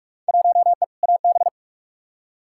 Morse code, heard as a keyed tone, about 41 wpm, 700 Hz. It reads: 1E AB
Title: Field Day Exchanges - 45wpm